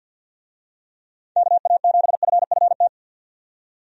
Morse code, heard as a keyed tone, about 50 wpm, 700 Hz. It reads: KA7FFT
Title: USA Call Signs - 6 Characters - 50wpm